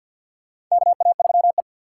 Morse code, heard as a keyed tone, about 52 wpm, 700 Hz. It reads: KA3E
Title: USA Call Signs - 2x Character Spacing - 50wpm